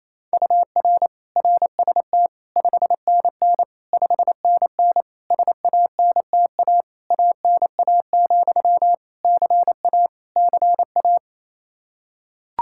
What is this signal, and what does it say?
Morse code, heard as a keyed tone, about 28 wpm, 700 Hz. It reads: UR RST 5NN 5NN SANTA ANA, CA CA
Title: Practice QSO - Exchange 1 - Beginning - 28wpm